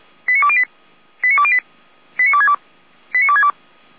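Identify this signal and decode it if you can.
Five-tone selective call (ZVEI), heard as quick runs of five tones, tones of about 70 ms. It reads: 89208 89208 89262 89262
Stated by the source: ZVEI five-tone call received on an SDR, channel 455.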